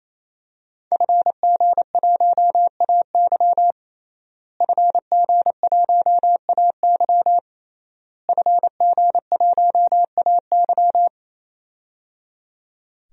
Morse code, heard as a keyed tone, about 28 wpm, 700 Hz. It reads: FG1AY FG1AY FG1AY
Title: DXCC-100 Call Signs - 5 Characters - Repeated 3x - 28wpm